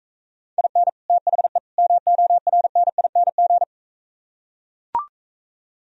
Morse code, heard as a keyed tone, about 45 wpm, 700 Hz. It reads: IN THE MORNING